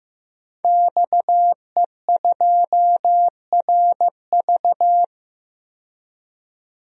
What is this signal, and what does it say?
Morse code, heard as a keyed tone, about 15 wpm, 700 Hz. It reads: XE2RV
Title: DXCC-100 Call Signs - 5 Characters - 15wpm